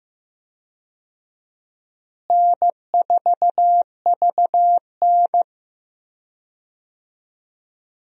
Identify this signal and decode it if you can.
Morse code, heard as a keyed tone, about 16 wpm, 700 Hz. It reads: N4VN
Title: USA Call Signs - 4 Characters - Repeated 2x - 15wpm